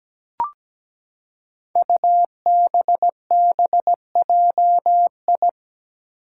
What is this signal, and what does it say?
Morse code, heard as a keyed tone, about 17 wpm, 700 Hz. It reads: UBBJI
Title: Five Challenging Letter-Number Combinations 17wpm